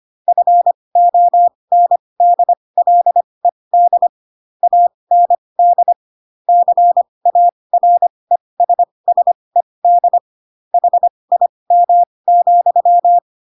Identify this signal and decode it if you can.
Morse code, heard as a keyed tone, about 25 wpm, spaced as for 20 wpm, 700 Hz. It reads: FONDLED AND CARESSED HIM,
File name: Pinocchio / Pinocchio ch7 15 25 ik1hge